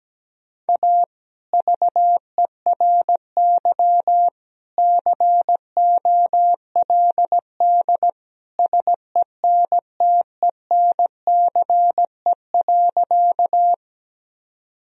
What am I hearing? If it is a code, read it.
Morse code, heard as a keyed tone, about 17 wpm, 700 Hz. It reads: A VERY COLD SENTENCE.
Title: Sentences from Top 1000 Words - Review Part 1 - 17wpm